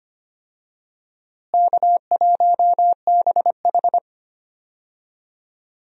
Morse code, heard as a keyed tone, about 25 wpm, 700 Hz. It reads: K1BH
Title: USA Call Signs - 4 Characters - Repeated 3x - 25wpm